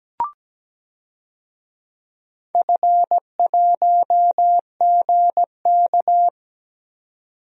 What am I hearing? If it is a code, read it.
Morse code, heard as a keyed tone, about 17 wpm, 700 Hz. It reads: F1GK
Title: DXCC-100 Call Signs - 4 Characters - 17wpm